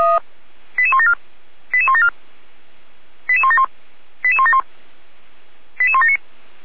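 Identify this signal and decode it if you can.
Five-tone selective call (ZVEI), heard as quick runs of five tones, tones of about 70 ms. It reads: 88174 88174 88171 88171 88179
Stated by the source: ZVEI five-tone call received by SDR.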